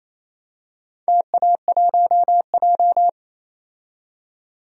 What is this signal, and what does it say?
Morse code, heard as a keyed tone, about 27 wpm, 700 Hz. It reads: TA1J
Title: DXCC-100 Call Signs - 4 Characters - 28wpm